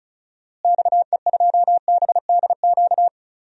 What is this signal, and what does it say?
Morse code, heard as a keyed tone, about 34 wpm, 700 Hz. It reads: XE2BDQ
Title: DXCC-100 Call Signs - 6 Characters - Repeated 2x - 35wpm